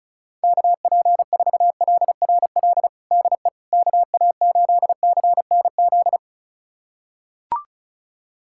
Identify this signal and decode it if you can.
Morse code, heard as a keyed tone, about 36 wpm, 690 Hz. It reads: KP4LRL DE KA8CNZ
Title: USA Callsign DE Callsign 35wpm